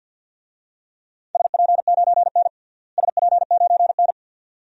Morse code, heard as a keyed tone, about 48 wpm, 700 Hz. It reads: SP9N SP9N
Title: DXCC-100 Call Signs - 4 Characters - Repeated 2x - 50wpm